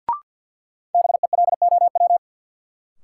Morse code, heard as a keyed tone, about 46 wpm, 700 Hz. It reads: BELOW
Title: Sentences from Top 1000 Words - Review Part 5 - New Word Drill - 50wpm